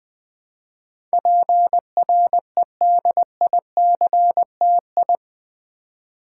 Morse code, heard as a keyed tone, about 20 wpm, 700 Hz. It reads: PREDICTI
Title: Build A Word - 4 Letters At A Time - 10-Letter Words - 20wpm